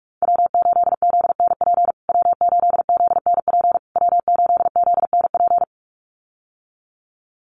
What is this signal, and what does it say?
Morse code, heard as a keyed tone, about 44 wpm, 700 Hz. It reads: W8ZNP W8ZNP W8ZNP
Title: USA Call Signs Repeated 3x 45wpm